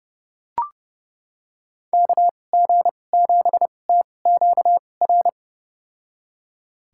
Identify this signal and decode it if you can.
Morse code, heard as a keyed tone, about 30 wpm, spaced as for 22 wpm, 700 Hz. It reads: KG7TQR
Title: USA Call Signs - 6 Characters - 2x Character Spacing - 30wpm